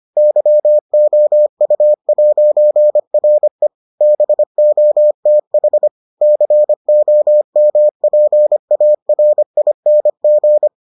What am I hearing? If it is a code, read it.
Morse code, heard as a keyed tone, about 25 wpm, 600 Hz. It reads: YOU'RE BOTH COMPARING